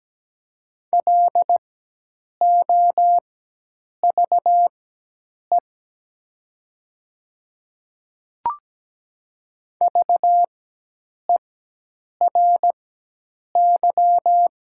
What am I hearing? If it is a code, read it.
Morse code, heard as a keyed tone, about 17 wpm, spaced as for 8 wpm, 700 Hz. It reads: LOVE VERY
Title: 4x Character Spacing - Top 500 Words - 17wpm